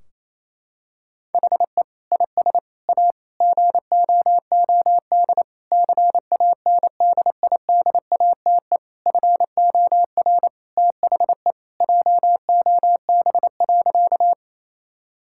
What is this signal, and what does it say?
Morse code, heard as a keyed tone, about 28 wpm, 700 Hz. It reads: HE IS A GOOD CANDIDATE FOR THE JOB.
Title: Sentences from Top 1600 Words 28wpm